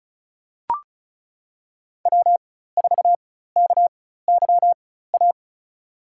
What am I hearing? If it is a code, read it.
Morse code, heard as a keyed tone, about 34 wpm, spaced as for 16 wpm, 700 Hz. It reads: W4KYA
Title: USA Call Signs - 5 Characters - 4x Character Spacing - 35wpm